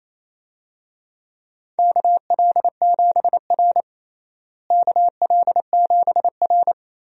Morse code, heard as a keyed tone, about 28 wpm, 700 Hz. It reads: KL7R KL7R
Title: DXCC-100 Call Signs - 4 Characters - Repeated 2x - 28wpm